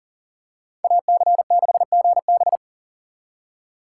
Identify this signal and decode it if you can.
Morse code, heard as a keyed tone, about 40 wpm, 700 Hz. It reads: AC6GB
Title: USA Call Signs - 2x Character Spacing - 40wpm